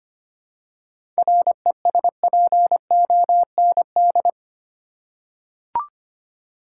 Morse code, heard as a keyed tone, about 25 wpm, 700 Hz. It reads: RESPOND
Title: Seven Letter Words - 25wpm